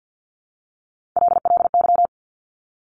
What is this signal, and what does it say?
Morse code, heard as a keyed tone, about 50 wpm, 700 Hz. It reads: LLY
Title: Top Three Letter Combinations in Words 50wpm